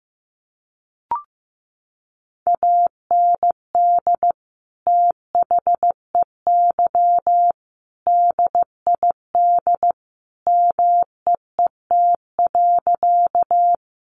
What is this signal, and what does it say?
Morse code, heard as a keyed tone, about 15 wpm, 700 Hz. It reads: AND THEY DID MEET.
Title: Sentences from Top 1100 Words 15wpm